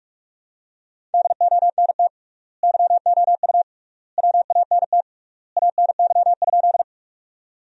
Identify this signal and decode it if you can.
Morse code, heard as a keyed tone, about 46 wpm, 700 Hz. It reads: DONT YOU WANT ANY?
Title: 3x Word-Spacing - Sentences from Top 300 Words - 45wpm